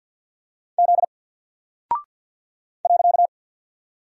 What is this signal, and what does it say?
Morse code, heard as a keyed tone, about 50 wpm, 700 Hz. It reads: Z .
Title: Single Character (Letter:Number:Punctuation:Prosign) 50wpm